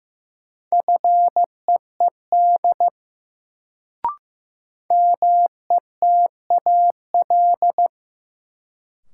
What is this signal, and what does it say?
Morse code, heard as a keyed tone, about 15 wpm, 700 Hz. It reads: FEED METAL
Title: Sentences from Top 1600 Words - New Word Drill Alt - 15wpm